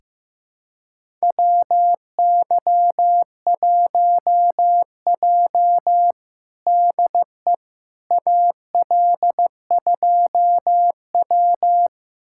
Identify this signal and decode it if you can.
Morse code, heard as a keyed tone, about 15 wpm, 700 Hz. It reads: WY1J DE AL2W
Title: USA Callsign DE Callsign - 4 Characters - 15wpm